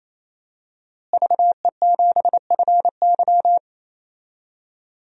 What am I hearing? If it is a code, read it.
Morse code, heard as a keyed tone, about 27 wpm, 700 Hz. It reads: VE7FY
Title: DXCC-100 Call Signs - 5 Characters - Repeated 3x - 28wpm